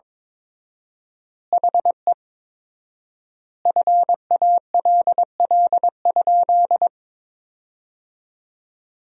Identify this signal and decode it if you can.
Morse code, heard as a keyed tone, about 22 wpm, 700 Hz. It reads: HE FALL?
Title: 4x Word-Spacing - Sentences from Top 500 Words - 22wpm